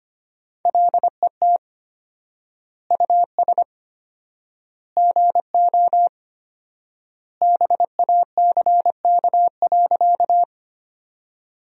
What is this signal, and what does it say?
Morse code, heard as a keyed tone, about 25 wpm, 700 Hz. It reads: LET US GO BACK.
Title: 4x Word-Spacing - Sentences from Top 300 Words - 25wpm